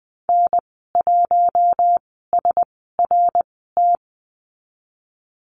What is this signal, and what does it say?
Morse code, heard as a keyed tone, about 19 wpm, spaced as for 14 wpm, 700 Hz. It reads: N1SRT
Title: 069 Lesson 12g - Letter d - Call Signs 20wpm